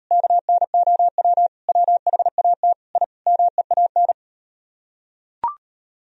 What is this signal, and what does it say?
Morse code, heard as a keyed tone, about 38 wpm, 700 Hz. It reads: KNOW WHAT I MEAN